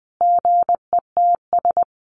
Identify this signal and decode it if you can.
Morse code, heard as a keyed tone, about 20 wpm, 700 Hz. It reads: GETS
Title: Sentences from Top 1300 Words 20wpm